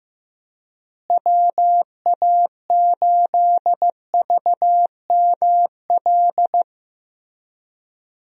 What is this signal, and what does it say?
Morse code, heard as a keyed tone, about 15 wpm, 700 Hz. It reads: WA8VML
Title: USA Call Signs - 6 Characters - Repeated 3x - 15wpm